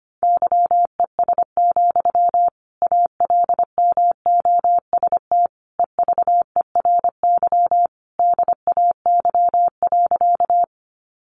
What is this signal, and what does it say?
Morse code, heard as a keyed tone, about 25 wpm, 700 Hz. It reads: YES, ALMOST EVERY DAY.
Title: Sentences from Top 700 Words 25wpm